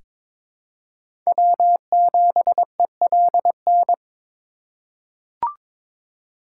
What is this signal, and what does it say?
Morse code, heard as a keyed tone, about 21 wpm, 700 Hz. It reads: W7ELN
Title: USA Call Signs - 3x Character Spacing - 22wpm